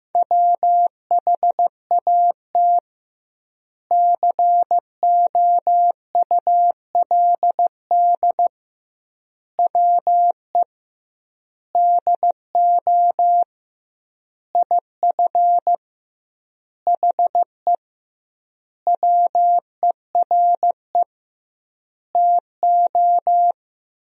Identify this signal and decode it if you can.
Morse code, heard as a keyed tone, about 15 wpm, 700 Hz. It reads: WHAT COULD WE DO IF HE WERE TO
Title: 2x Word-Spacing - Sentences from Top 400 Words - 15wpm